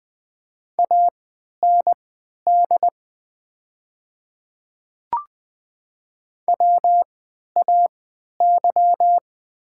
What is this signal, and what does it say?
Morse code, heard as a keyed tone, about 20 wpm, spaced as for 11 wpm, 690 Hz. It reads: AND WAY
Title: 3x Character Spacing - Top 100 Words - 20wpm